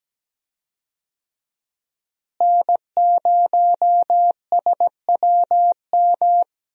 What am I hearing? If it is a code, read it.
Morse code, heard as a keyed tone, about 17 wpm, 700 Hz. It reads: N0SWM
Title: USA Call Signs - 5 Characters - Repeated 2x - 17wpm